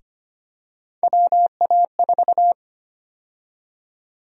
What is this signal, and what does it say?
Morse code, heard as a keyed tone, about 25 wpm, 700 Hz. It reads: WA4
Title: USA Call Sign Prefixes - 25wpm